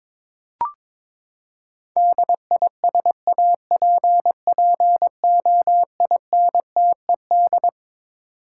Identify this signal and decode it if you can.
Morse code, heard as a keyed tone, about 22 wpm, 700 Hz. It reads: DISAPPOINTED
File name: Twelve Letter Words - 22wpm